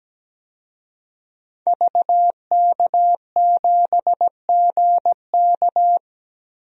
Morse code, heard as a keyed tone, about 17 wpm, 700 Hz. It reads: VK7GK
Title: DXCC-100 Call Signs - 5 Characters - 17wpm